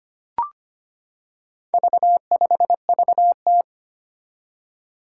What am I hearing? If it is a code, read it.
Morse code, heard as a keyed tone, about 25 wpm, 700 Hz. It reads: V5VT
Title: Four Challenging Letter-Number Combinations 25wpm